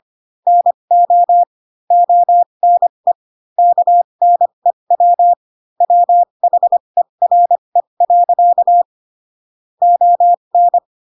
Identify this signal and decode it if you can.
Morse code, heard as a keyed tone, about 25 wpm, spaced as for 22 wpm, 700 Hz. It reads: NO ONE KNEW WHERE. ON
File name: Pinocchio / Pinocchio ch2 18 25 ik1hge